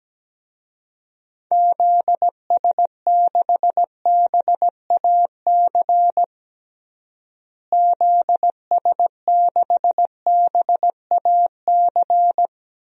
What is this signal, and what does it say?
Morse code, heard as a keyed tone, about 17 wpm, 700 Hz. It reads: ZS6BAC ZS6BAC
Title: DXCC-100 Call Signs - 6 Characters - Repeated 3x - 17wpm